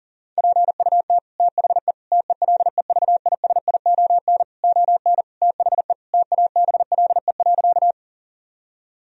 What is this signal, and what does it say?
Morse code, heard as a keyed tone, about 40 wpm, 700 Hz. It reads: PUT THE TELEVISION ON THE TABLE.
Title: Sentences from Top 1000 Words - Review Part 4 - 40wpm